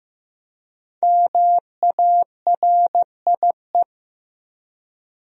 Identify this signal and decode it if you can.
Morse code, heard as a keyed tone, about 15 wpm, 700 Hz. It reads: MARIE
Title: USA Common Names - Max 5 Letters - 15wpm